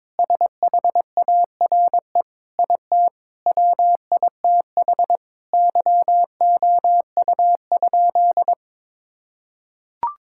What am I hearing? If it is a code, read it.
Morse code, heard as a keyed tone, about 22 wpm, 700 Hz. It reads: SHARE IT WITH YOU?
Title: Sentences from Top 1000 Words - Review Part 6 - 22wpm